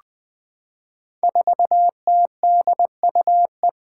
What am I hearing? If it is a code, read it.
Morse code, heard as a keyed tone, about 20 wpm, 700 Hz. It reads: 4TDUE
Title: Five Challenging Letter-Number Combinations 20wpm